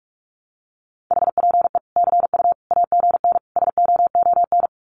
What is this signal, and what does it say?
Morse code, heard as a keyed tone, about 44 wpm, 700 Hz. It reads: HPE CU AGN SOON
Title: Common QSO Phrases 45wpm